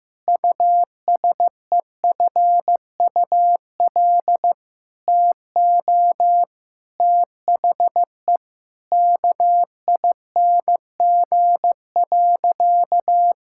Morse code, heard as a keyed tone, about 15 wpm, 700 Hz. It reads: USEFUL TO THE KING.
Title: Sentences from Top 1000 Words - Review Part 6 - 15wpm